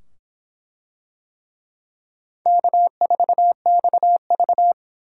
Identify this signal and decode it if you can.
Morse code, heard as a keyed tone, about 23 wpm, 700 Hz. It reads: K4XV